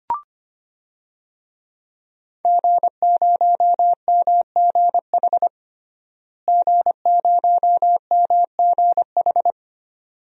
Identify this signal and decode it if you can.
Morse code, heard as a keyed tone, about 25 wpm, 700 Hz. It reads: G0MGH G0MGH
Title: DXCC-100 Call Signs - 5 Characters - Repeated 2x - 25wpm